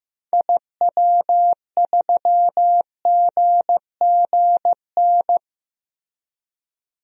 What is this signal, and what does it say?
Morse code, heard as a keyed tone, about 15 wpm, 700 Hz. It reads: IW3GGN
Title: DXCC-100 Call Signs - 6 Characters - Repeated 2x - 15wpm